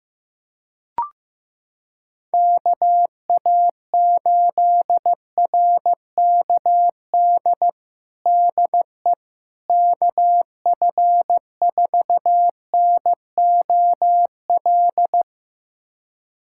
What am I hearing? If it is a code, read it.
Morse code, heard as a keyed tone, about 15 wpm, 700 Hz. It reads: KA8RKD DE KF4NOL
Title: USA Callsign DE Callsign - 6 Characters - 15wpm